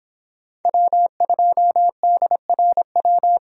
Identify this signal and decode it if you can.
Morse code, heard as a keyed tone, about 24 wpm, 700 Hz. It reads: W2DRW